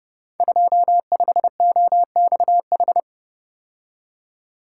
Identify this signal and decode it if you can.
Morse code, heard as a keyed tone, about 30 wpm, 700 Hz. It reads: 25OXH
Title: Code Groups - 5 Letter:Numbers - 30wpm